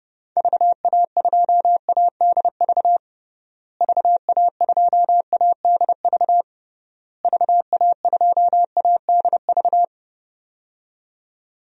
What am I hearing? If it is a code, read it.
Morse code, heard as a keyed tone, about 30 wpm, 700 Hz. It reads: VA2ADV VA2ADV VA2ADV
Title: DXCC-100 Call Signs - 6 Characters - Repeated 3x - 30wpm